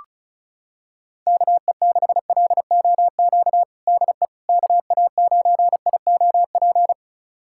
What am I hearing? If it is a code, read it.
Morse code, heard as a keyed tone, about 36 wpm, 700 Hz. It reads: KE6LOQ DE KA9IOP
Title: USA Callsign DE Callsign 35wpm